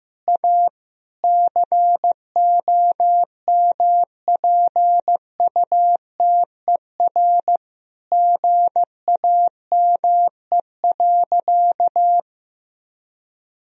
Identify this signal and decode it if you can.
Morse code, heard as a keyed tone, about 15 wpm, 700 Hz. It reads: A COMPUTER GAME.
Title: Sentences from Top 1000 Words - Review Part 3 - 15wpm